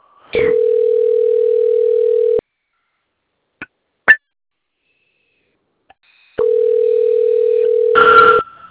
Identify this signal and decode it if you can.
Ringback tone (North America), 440 + 480 Hz.